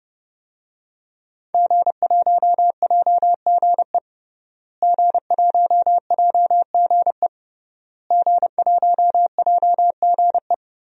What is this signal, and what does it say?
Morse code, heard as a keyed tone, about 30 wpm, 700 Hz. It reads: G1JGE G1JGE G1JGE
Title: DXCC-100 Call Signs - 5 Characters - Repeated 3x - 30wpm